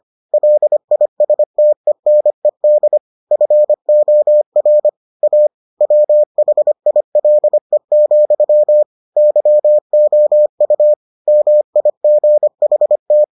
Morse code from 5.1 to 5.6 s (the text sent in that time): A